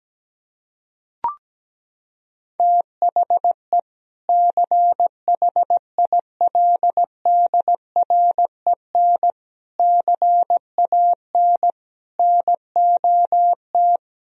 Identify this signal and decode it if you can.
Morse code, heard as a keyed tone, about 17 wpm, 700 Hz. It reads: THE CHILDREN CAN NOT
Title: Sentences from Top 1000 Words - Review Part 4 - 17wpm